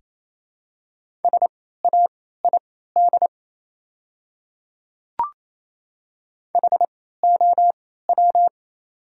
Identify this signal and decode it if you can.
Morse code, heard as a keyed tone, about 28 wpm, spaced as for 16 wpm, 700 Hz. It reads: SAID HOW
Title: 3x Character Spacing - Top 100 Words - 28wpm